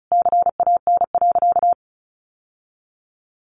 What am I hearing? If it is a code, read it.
Morse code, heard as a keyed tone, about 34 wpm, 700 Hz. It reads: CAN.
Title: Sentences from Top 600 Words 35wpm